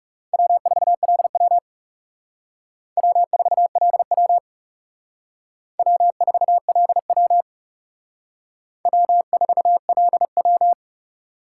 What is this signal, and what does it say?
Morse code, heard as a keyed tone, about 37 wpm, 700 Hz. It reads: W4LW W4LW W4LW W4LW
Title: Speed Racing - USA 1x2 Call Signs 30wpm